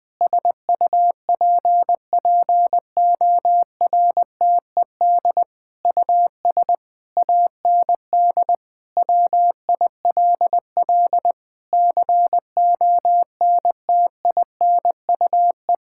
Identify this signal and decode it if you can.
Morse code, heard as a keyed tone, about 21 wpm, 700 Hz. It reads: SUPPORTED US AND WILL CONTINUE
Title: Sentences with 6-Word Phrases - 20wpm